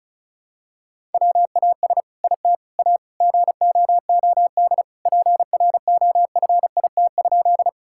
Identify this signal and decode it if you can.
Morse code, heard as a keyed tone, about 36 wpm, 700 Hz. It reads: WAS IT A GOOD PROFIT?
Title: Sentences from Top 1000 Words - Review Part 5 - 35wpm